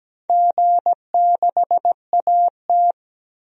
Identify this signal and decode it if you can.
Morse code, heard as a keyed tone, about 17 wpm, 700 Hz. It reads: G6AT
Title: DXCC-100 Call Signs - 4 Characters - 17wpm